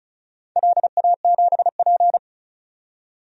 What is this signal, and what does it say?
Morse code, heard as a keyed tone, about 34 wpm, 700 Hz. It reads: LA7P
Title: DXCC-100 Call Signs - 4 Characters - 35wpm